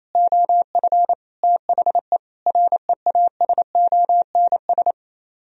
Morse code, heard as a keyed tone, about 33 wpm, 700 Hz. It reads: OF THE REASONS
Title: Speed-Racing - Sets of 4 Words 28wpm